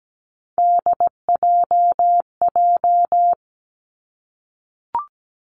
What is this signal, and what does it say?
Morse code, heard as a keyed tone, about 17 wpm, 700 Hz. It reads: DJJ
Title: Three Challenging Letter-Number Combinations 17wpm